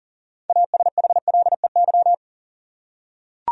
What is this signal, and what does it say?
Morse code, heard as a keyed tone, about 40 wpm, 700 Hz. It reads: ASHLEY
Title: USA Common Names - 40wpm